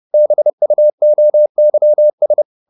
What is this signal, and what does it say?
Morse code, heard as a keyed tone, about 30 wpm, 600 Hz. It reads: BUOYS
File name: the old man and the sea / the old man and the sea 1 30wpm